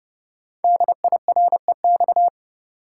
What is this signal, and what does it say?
Morse code, heard as a keyed tone, about 29 wpm, 700 Hz. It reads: DIREX
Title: Code Groups - 5 Letters - 30wpm